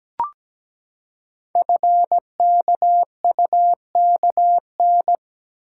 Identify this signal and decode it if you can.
Morse code, heard as a keyed tone, about 17 wpm, 700 Hz. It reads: FKUKN
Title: Code Groups - 5 Letters - 17wpm